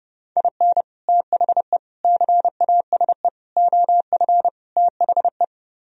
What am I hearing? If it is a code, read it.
Morse code, heard as a keyed tone, about 30 wpm, 700 Hz. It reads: IN THE CASE OF THE